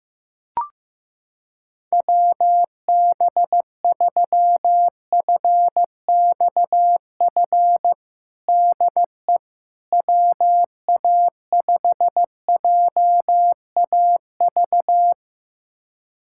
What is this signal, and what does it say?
Morse code, heard as a keyed tone, about 15 wpm, 700 Hz. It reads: WB3FXF DE WA5JAV
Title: USA Callsign DE Callsign 15wpm